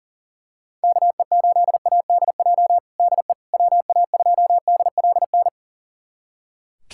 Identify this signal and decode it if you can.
Morse code, heard as a keyed tone, about 40 wpm, 700 Hz. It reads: KE8ADJ DE WA2DLN
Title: USA Callsign DE Callsign - 6 Characters - 40wpm